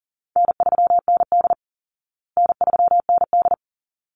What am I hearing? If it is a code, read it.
Morse code, heard as a keyed tone, about 40 wpm, 700 Hz. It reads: N3ND N3ND
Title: Easy Call Signs Repeated 2x - 40wpm